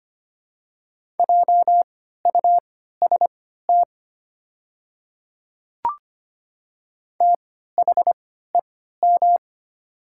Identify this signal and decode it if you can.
Morse code, heard as a keyed tone, about 25 wpm, spaced as for 14 wpm, 690 Hz. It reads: JUST THEM
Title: 3x Character Spacing - Top 200 Words - 25wpm